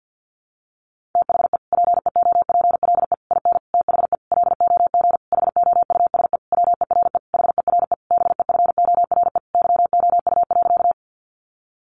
Morse code, heard as a keyed tone, about 50 wpm, 700 Hz. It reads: THE PEOPLE IN THE LOG HOUSE WERE HERE BEFORE YOU.
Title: Sentences from Top 1400 Words 50wpm